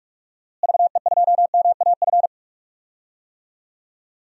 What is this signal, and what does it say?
Morse code, heard as a keyed tone, about 41 wpm, 700 Hz. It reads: VE2MAF
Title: DXCC-100 Call Signs - 6 Characters - Repeated 3x - 45wpm